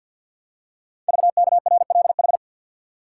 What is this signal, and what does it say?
Morse code, heard as a keyed tone, about 45 wpm, 700 Hz. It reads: VKRLH
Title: Code Groups - 5 Letters - 50wpm